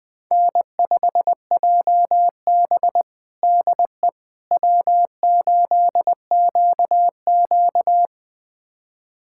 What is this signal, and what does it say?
Morse code, heard as a keyed tone, about 21 wpm, 700 Hz. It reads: N5JB DE W8QQ
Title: USA Callsign DE Callsign - 4 Characters - 20wpm